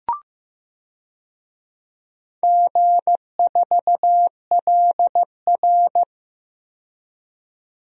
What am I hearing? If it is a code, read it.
Morse code, heard as a keyed tone, about 15 wpm, 700 Hz. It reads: G4LR
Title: DXCC-100 Call Signs - 4 Characters - 15wpm